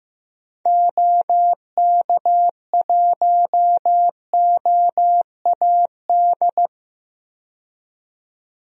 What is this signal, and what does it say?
Morse code, heard as a keyed tone, about 15 wpm, 700 Hz. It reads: OK1OAD
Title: DXCC-100 Call Signs - 6 Characters - 15wpm